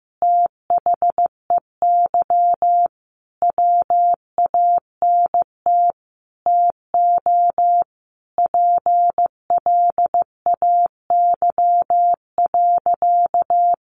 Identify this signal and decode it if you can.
Morse code, heard as a keyed tone, about 15 wpm, 700 Hz. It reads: THEY WANT TO PLAY.
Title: Easy Phrases 15wpm